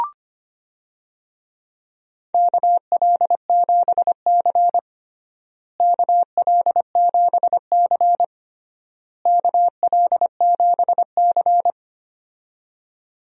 Morse code, heard as a keyed tone, about 25 wpm, 700 Hz. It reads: KL7C KL7C KL7C
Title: DXCC-100 Call Signs - 4 Characters - Repeated 3x - 25wpm